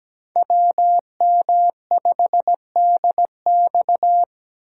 Morse code, heard as a keyed tone, about 17 wpm, 700 Hz. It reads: WM5DX
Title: K1USN SST QSO Practice - 17wpm